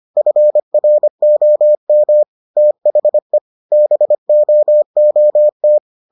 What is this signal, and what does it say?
Morse code, heard as a keyed tone, about 25 wpm, 600 Hz. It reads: FROM THE BOOT